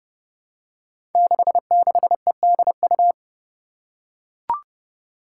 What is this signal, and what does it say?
Morse code, heard as a keyed tone, about 30 wpm, 700 Hz. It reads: 66EDU
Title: Five Challenging Letter-Number Combinations 30wpm